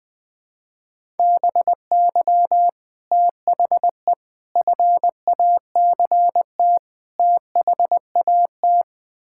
Morse code, heard as a keyed tone, about 21 wpm, 700 Hz. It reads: BY THE FACT THAT
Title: Build A Phrase - 4-Word Phrases - 20wpm